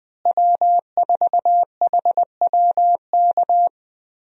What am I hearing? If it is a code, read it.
Morse code, heard as a keyed tone, about 20 wpm, 700 Hz. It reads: W4HWK
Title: USA Call Signs - 5 Characters - Repeated 2x - 20wpm